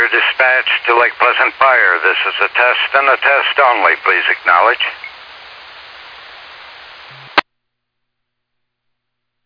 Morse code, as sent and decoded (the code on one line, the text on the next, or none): none